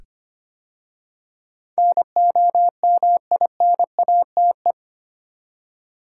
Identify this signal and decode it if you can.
Morse code, heard as a keyed tone, about 25 wpm, 700 Hz. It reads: NOMINATE
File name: 8 Letter Words - First 3 Letters Given - 25wpm